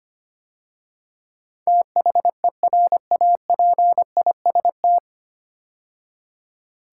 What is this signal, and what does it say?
Morse code, heard as a keyed tone, about 25 wpm, 700 Hz. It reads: THERAPIST
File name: practice / 25wpm / 9 Letter Words - First 3 Letters Given - 25wpm